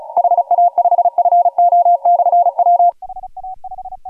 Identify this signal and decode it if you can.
Morse code, heard as a keyed tone, about 36 wpm, 700 Hz. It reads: HA5FO/W
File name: AUDIO2018 / HG1G CW